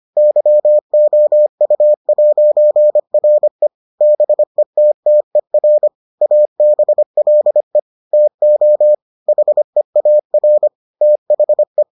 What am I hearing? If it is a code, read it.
Morse code, heard as a keyed tone, about 25 wpm, 600 Hz. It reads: YOU'RE BETTER ABLE TO HEAR THE